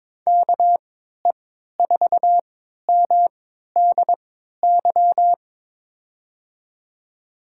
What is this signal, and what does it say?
Morse code, heard as a keyed tone, about 22 wpm, spaced as for 13 wpm, 700 Hz. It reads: KE4MDY
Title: USA Call Signs - 6 Characters - 3x Character Spacing - 22wpm